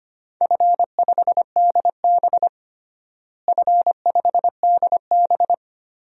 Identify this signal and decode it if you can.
Morse code, heard as a keyed tone, about 25 wpm, 700 Hz. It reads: F5DB F5DB
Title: DXCC-100 Call Signs - 4 Characters - Repeated 2x - 25wpm